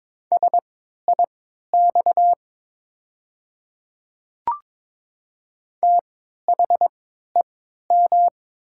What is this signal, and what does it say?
Morse code, heard as a keyed tone, about 22 wpm, spaced as for 12 wpm, 700 Hz. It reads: SIX THEM
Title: 3x Character Spacing - Top 500 Words - 22wpm